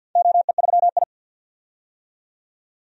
Morse code, heard as a keyed tone, about 51 wpm, 700 Hz. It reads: OE3I
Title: DXCC-100 Call Signs - 4 Characters - 50wpm